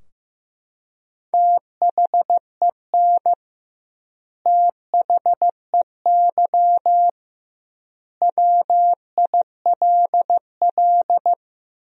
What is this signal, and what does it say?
Morse code, heard as a keyed tone, about 15 wpm, 700 Hz. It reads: THEN THEY WILL
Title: 2x Word-Spacing - Sentences from Top 200 Words - 15wpm